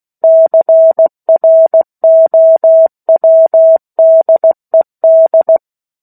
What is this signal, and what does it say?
Morse code, heard as a keyed tone, about 16 wpm, 650 Hz. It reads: CROWDED
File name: 03-Chapter-03 16 WPM